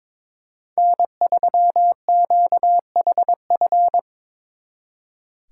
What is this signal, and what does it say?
Morse code, heard as a keyed tone, about 25 wpm, 700 Hz. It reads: N3QHF
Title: Speed Racing - USA 1x3 Call Signs 22wpm